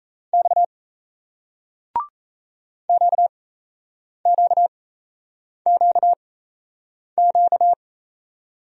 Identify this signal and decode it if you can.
Morse code, heard as a keyed tone, about 37 wpm, spaced as for 8 wpm, 700 Hz. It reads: X QQQQ